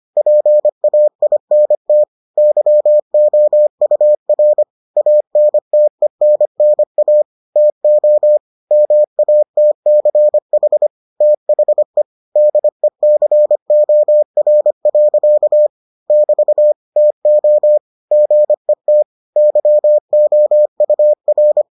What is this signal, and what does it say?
Morse code, heard as a keyed tone, about 25 wpm, 600 Hz. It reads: PAINT YOUR ANTENNA TO MATCH THE DECOR. <BT> TO GET YOUR